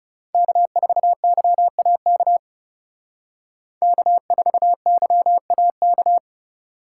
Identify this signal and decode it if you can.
Morse code, heard as a keyed tone, about 37 wpm, 700 Hz. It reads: K4YAK K4YAK
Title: Speed Racing - USA 1x3 Call Signs 30wpm